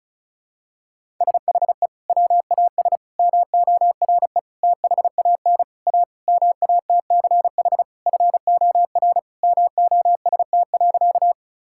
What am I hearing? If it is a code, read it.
Morse code, heard as a keyed tone, about 36 wpm, 700 Hz. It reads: SHE WAS MORE THAN A MATCH FOR MOST.
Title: Sentences from Top 1600 Words 35wpm